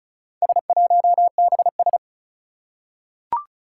Morse code, heard as a keyed tone, about 36 wpm, 700 Hz. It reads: S1BS
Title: Four Challenging Letter-Number Combinations 35wpm